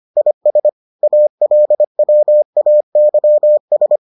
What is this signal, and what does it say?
Morse code, heard as a keyed tone, about 25 wpm, 600 Hz. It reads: IS ALWAYS